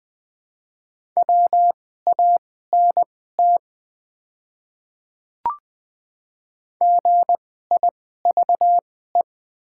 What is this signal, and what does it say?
Morse code, heard as a keyed tone, about 20 wpm, spaced as for 15 wpm, 700 Hz. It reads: WANT GIVE
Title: 2x Character Spacing - Top 200 Words - 20wpm